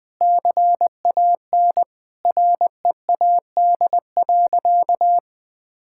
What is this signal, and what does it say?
Morse code, heard as a keyed tone, about 20 wpm, 700 Hz. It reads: CAN READ.
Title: Sentences with 8-Word Phrases - Spoken Beginning - 20wpm